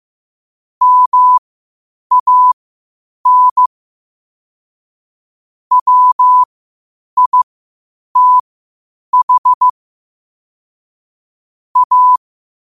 Morse code, heard as a keyed tone, about 15 wpm, spaced as for 8 wpm, 1000 Hz. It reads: MAN WITH A
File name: qotdfiles / 8wpm